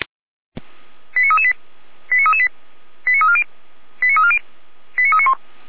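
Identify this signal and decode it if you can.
Five-tone selective call (ZVEI), heard as quick runs of five tones, tones of about 70 ms. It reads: 89338 89338 89340 89340 89391
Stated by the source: ZVEI five-tone call received on SDR, from the Feuerwehr Orlamünde (fire brigade) audio server.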